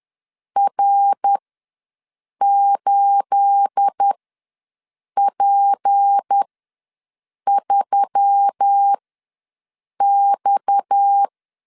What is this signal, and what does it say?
Morse code, heard as a keyed tone, about 10 wpm, spaced as for 6 wpm, 800 Hz. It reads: R8P3X
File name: LEZIONI / 9 28 ws 10wpm 4124